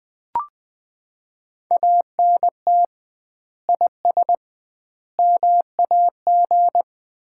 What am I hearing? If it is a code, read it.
Morse code, heard as a keyed tone, about 20 wpm, 700 Hz. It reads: ANT IS MAG
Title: Common QSO Phrases - 2x Word-Spacing - 20wpm